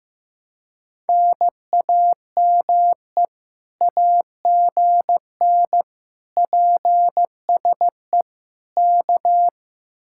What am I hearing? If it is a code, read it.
Morse code, heard as a keyed tone, about 15 wpm, 700 Hz. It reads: NAME AGN PSE K
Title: Speed-Racing Common QSO Phrases Encore - 15wpm